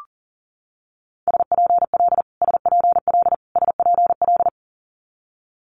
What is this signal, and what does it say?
Morse code, heard as a keyed tone, about 43 wpm, 700 Hz. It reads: SPL SPL SPL
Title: Multi-letter Phonemes Repeated 3x 40wpm